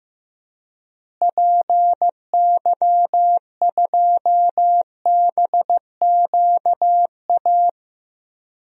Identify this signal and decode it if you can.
Morse code, heard as a keyed tone, about 15 wpm, 700 Hz. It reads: PY2BQA
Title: DXCC-100 Call Signs - 6 Characters - Repeated 3x - 15wpm